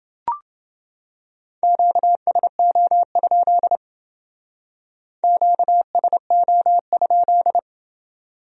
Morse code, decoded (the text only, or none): QSO? QSO?